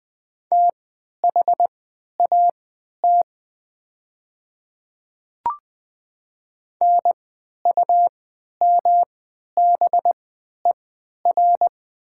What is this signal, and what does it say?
Morse code, heard as a keyed tone, about 20 wpm, spaced as for 11 wpm, 700 Hz. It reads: THAT NUMBER
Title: 3x Character Spacing - Top 200 Words - 20wpm